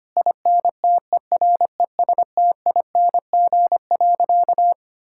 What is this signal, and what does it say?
Morse code, heard as a keyed tone, about 25 wpm, 700 Hz. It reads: INTERESTING.
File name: practice / 25wpm / Sentences with 2-Word Phrases - 25wpm